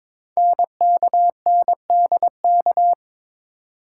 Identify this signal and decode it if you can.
Morse code, heard as a keyed tone, about 22 wpm, 700 Hz. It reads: NKNDK
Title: Code Groups - 5 Letters - 22wpm